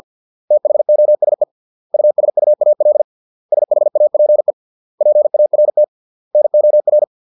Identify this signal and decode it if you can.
Morse code, heard as a keyed tone, about 50 wpm, 600 Hz. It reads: THOSE USUAL SHAPE PART NOR